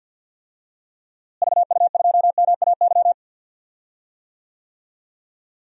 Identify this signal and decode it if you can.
Morse code, heard as a keyed tone, about 50 wpm, 700 Hz. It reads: VU2MAY
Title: DXCC-100 Call Signs - 6 Characters - Repeated 2x - 50wpm